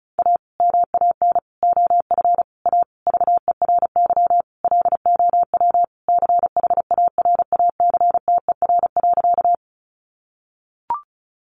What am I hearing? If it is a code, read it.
Morse code, heard as a keyed tone, about 34 wpm, 700 Hz. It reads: A MAN OF A VERY LOW CHARACTER.
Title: Sentences from Top 1200 Words 35wpm